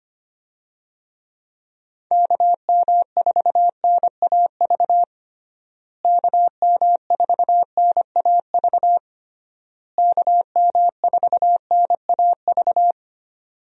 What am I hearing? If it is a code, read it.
Morse code, heard as a keyed tone, about 25 wpm, 700 Hz. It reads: KM4NAV KM4NAV KM4NAV
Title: USA Call Signs - 6 Characters - Repeated 3x - 25wpm